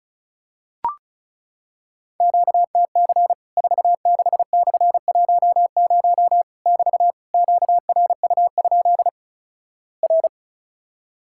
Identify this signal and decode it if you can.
Morse code, heard as a keyed tone, about 36 wpm, 700 Hz. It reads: QTC 46/10 <BT> QRU? R
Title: Worked All Europe DX Contest - QTC Exchange - 35wpm